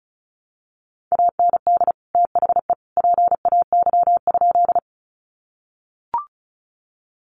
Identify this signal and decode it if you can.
Morse code, heard as a keyed tone, about 34 wpm, 700 Hz. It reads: AND THE PAY?
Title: Sentences from Top 700 Words 35wpm